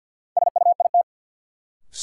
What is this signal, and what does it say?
Morse code, heard as a keyed tone, about 52 wpm, 700 Hz. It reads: SUIT
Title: Sentences from Top 1600 Words - New Word Drill Alt - 50wpm